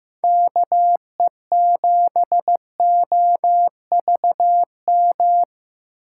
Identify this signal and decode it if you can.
Morse code, heard as a keyed tone, about 15 wpm, 700 Hz. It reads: KE7OVM
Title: USA Call Signs - 6 Characters - 6x Character Spacing - 15wpm